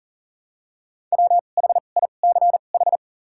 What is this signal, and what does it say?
Morse code, heard as a keyed tone, about 37 wpm, spaced as for 28 wpm, 700 Hz. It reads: WHICH
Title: 2x Character Spacing - Top 300 Words - 40wpm